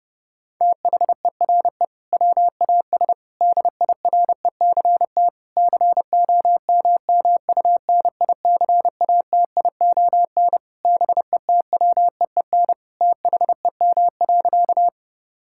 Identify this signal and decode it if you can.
Morse code, heard as a keyed tone, about 30 wpm, 700 Hz. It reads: THERE WAS DIRECT COMMUNICATION BETWEEN THEM.
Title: Sentences from Top 1000 Words - Review Part 6 - 30wpm